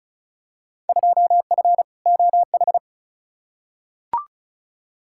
Morse code, heard as a keyed tone, about 36 wpm, 700 Hz. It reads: 2F OH
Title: Field Day Exchanges - 35wpm